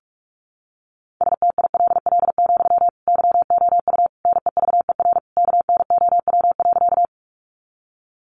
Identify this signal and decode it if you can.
Morse code, heard as a keyed tone, about 44 wpm, 700 Hz. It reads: STILL, YOU NEVER KNOW.
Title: Sentences from Top 300 Words 45wpm